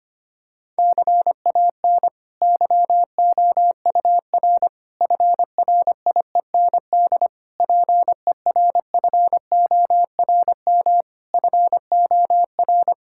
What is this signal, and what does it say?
Morse code, heard as a keyed tone, about 25 wpm, 700 Hz. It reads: CAN YOUR FRIEND PERFORM FOR
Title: Sentences from Top 1000 Words - Review Part 5 - 25wpm